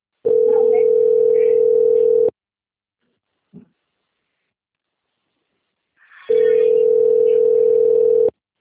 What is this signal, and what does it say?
Ringback tone (North America), 440 + 480 Hz.